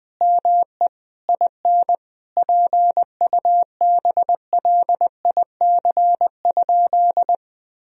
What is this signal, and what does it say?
Morse code, heard as a keyed tone, about 20 wpm, 700 Hz. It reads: ME IN PUBLIC?
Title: Sentences from Top 1000 Words - Review Part 3 - 20wpm